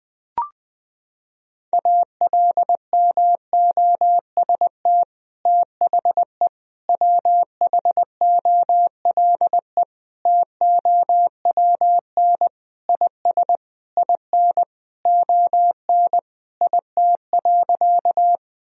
Morse code, heard as a keyed tone, about 20 wpm, 700 Hz. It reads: ALMOST THE WHOLE TOWN IS IN ON IT.
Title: Sentences from Top 1000 Words - Review Part 4 - 20wpm